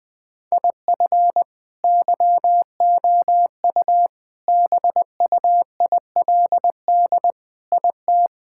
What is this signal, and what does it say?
Morse code, heard as a keyed tone, about 20 wpm, 700 Hz. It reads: IF YOU BUILD IT
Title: Sentences with 7-Word Phrases - 20wpm